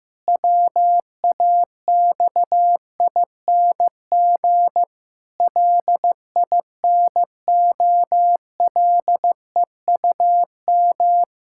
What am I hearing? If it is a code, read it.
Morse code, heard as a keyed tone, about 15 wpm, 700 Hz. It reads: WAXING LINOLEUM
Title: Pangram Challenge v2 - 15wpm